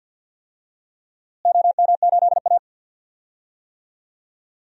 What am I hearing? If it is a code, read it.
Morse code, heard as a keyed tone, about 51 wpm, 700 Hz. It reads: OM8A
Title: DXCC-100 Call Signs - 4 Characters - Repeated 2x - 50wpm